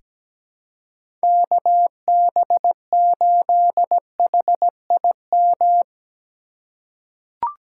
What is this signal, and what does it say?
Morse code, heard as a keyed tone, about 17 wpm, 700 Hz. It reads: KB8HIM
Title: USA Call Signs - 2x Character Spacing - 17wpm